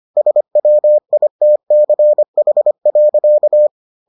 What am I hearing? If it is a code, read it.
Morse code, heard as a keyed tone, about 25 wpm, 600 Hz. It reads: SWITCH.